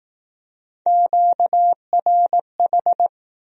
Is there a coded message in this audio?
Yes: Morse code from 0.9 to 3.1 s, heard as a keyed tone, about 19 wpm, 700 Hz. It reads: QRH